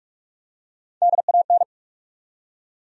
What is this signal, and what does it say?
Morse code, heard as a keyed tone, about 47 wpm, 700 Hz. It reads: DAN
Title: USA Common Names - Max 3 Letters - 45wpm